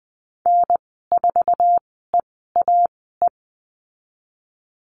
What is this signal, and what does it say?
Morse code, heard as a keyed tone, about 20 wpm, spaced as for 15 wpm, 700 Hz. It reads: N4EAE
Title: 088 Lesson 15g - Number 5 - Call Signs 20wpm